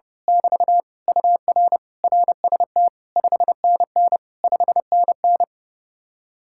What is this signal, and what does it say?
Morse code, heard as a keyed tone, about 29 wpm, 700 Hz. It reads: <BT> UR RST 5NN 5NN
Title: Speed-Racing Common QSO Phrases Encore - 30wpm